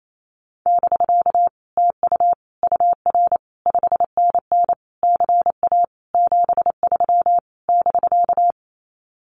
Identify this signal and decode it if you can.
Morse code, heard as a keyed tone, about 28 wpm, 700 Hz. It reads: <BK> TU UR 5NN CA 73 <BK>
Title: Common QSO Phrases Encore 28wpm